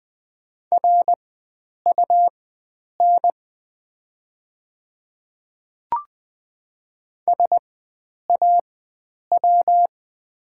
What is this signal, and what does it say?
Morse code, heard as a keyed tone, about 20 wpm, spaced as for 9 wpm, 700 Hz. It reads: RUN SAW
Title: 4x Character Spacing - Top 300 Words - 20wpm